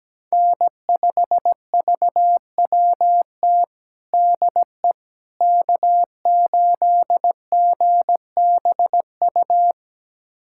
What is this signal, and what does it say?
Morse code, heard as a keyed tone, about 17 wpm, 700 Hz. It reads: N5VWT DE K8GBU
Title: USA Callsign DE Callsign - 5 Characters - 17wpm